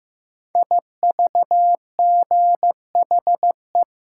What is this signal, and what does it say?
Morse code, heard as a keyed tone, about 15 wpm, 700 Hz. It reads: IVGHE
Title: Code Groups - 5 Letters - 15wpm